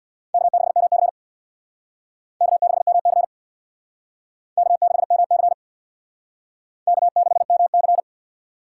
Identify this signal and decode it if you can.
Morse code, heard as a keyed tone, about 61 wpm, 700 Hz. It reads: K6MC K6MC K6MC K6MC
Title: Speed Racing - USA Call Signs 50wpm